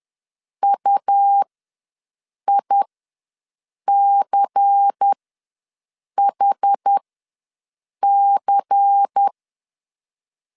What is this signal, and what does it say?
Morse code, heard as a keyed tone, about 10 wpm, spaced as for 6 wpm, 800 Hz. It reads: UICHC
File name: LEZIONI / 5 LBVC / 5 LBVC ws 10wpm 3733